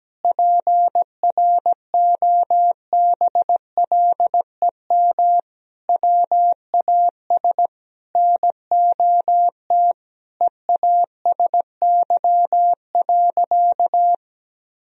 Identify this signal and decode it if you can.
Morse code, heard as a keyed tone, about 17 wpm, 700 Hz. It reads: PROBLEM WAS NOT EASY.
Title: Sentences from Top 1000 Words - Review Part 2 - 17wpm